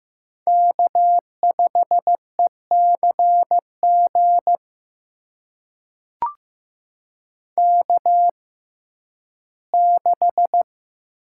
Morse code, heard as a keyed tone, about 15 wpm, spaced as for 11 wpm, 700 Hz. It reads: K5ECG K6
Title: USA Call Signs - 5 Characters - 6x Character Spacing - 15wpm